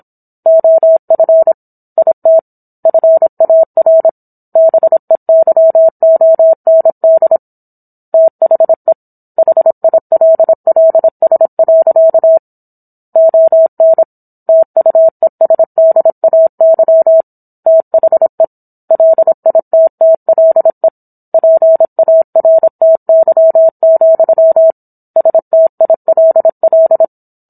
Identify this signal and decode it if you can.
Morse code, heard as a keyed tone, about 27 wpm, 650 Hz. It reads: OF IT FAR BEYOND THE HILLS. ON TUESDAY THE LITTLE PARTY, STILL